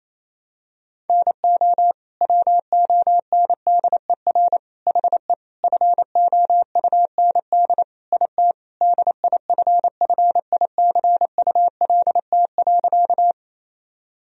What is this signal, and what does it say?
Morse code, heard as a keyed tone, about 28 wpm, 700 Hz. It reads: NO WONDER HE FOUND IT DIFFICULT.
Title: Sentences from Top 1000 Words - Review Part 3 - 28wpm